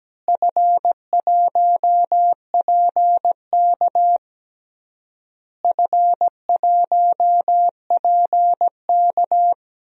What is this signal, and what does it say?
Morse code, heard as a keyed tone, about 17 wpm, 700 Hz. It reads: F1PK F1PK
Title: DXCC-100 Call Signs - 4 Characters - Repeated 3x - 17wpm